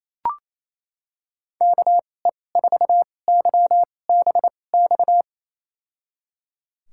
Morse code, heard as a keyed tone, about 28 wpm, spaced as for 20 wpm, 700 Hz. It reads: KE4YBX
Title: USA Call Signs - 6 Characters - 2x Character Spacing - 28wpm